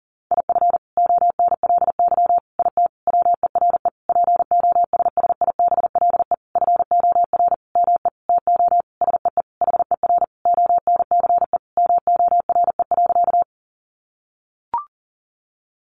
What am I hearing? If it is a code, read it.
Morse code, heard as a keyed tone, about 40 wpm, 700 Hz. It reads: IF ONLY IT WERE POSSIBLE FOR ME TO SEE HER ONCE MORE.
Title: Sentences from Top 500 Words 40wpm